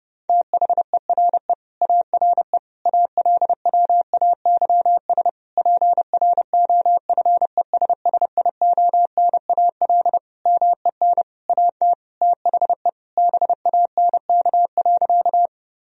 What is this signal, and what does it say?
Morse code, heard as a keyed tone, about 32 wpm, 700 Hz. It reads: THERE ARE ALWAYS PROFESSIONAL MEN AT THE BANK.
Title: Sentences from Top 1000 Words - Review Part 4 - 30wpm